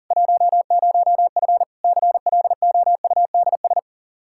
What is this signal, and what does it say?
Morse code, heard as a keyed tone, about 37 wpm, 700 Hz. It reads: 10F CLOUDS
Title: Practice QSO - Exchange 2 - Info - 40wpm